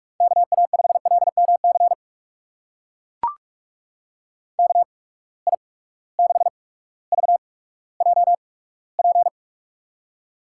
Morse code, heard as a keyed tone, about 43 wpm, spaced as for 19 wpm, 700 Hz. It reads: KA5LMC KI6VJP
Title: USA Call Signs - 6 Characters - 8x Character Spacing - 45wpm